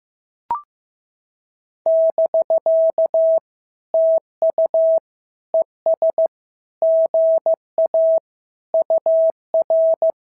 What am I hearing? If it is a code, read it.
Morse code, heard as a keyed tone, about 15 wpm, 650 Hz. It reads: <BK> TU ES GA UR
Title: SOTA QSO Practice 15wpm